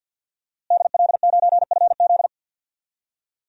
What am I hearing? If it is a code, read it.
Morse code, heard as a keyed tone, about 50 wpm, 700 Hz. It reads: DL9FZ
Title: DXCC-100 Call Signs - 5 Characters - Repeated 3x - 50wpm